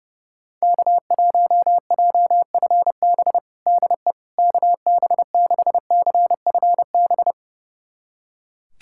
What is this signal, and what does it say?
Morse code, heard as a keyed tone, about 32 wpm, 700 Hz. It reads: K1JFB DE KB6CFB
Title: USA Callsign DE Callsign 30wpm